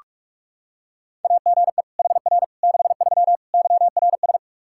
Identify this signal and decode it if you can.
Morse code, heard as a keyed tone, about 46 wpm, 700 Hz. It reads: AGE HR 63 YRS
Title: Practice QSO - Exchange 3 - More Info - 45wpm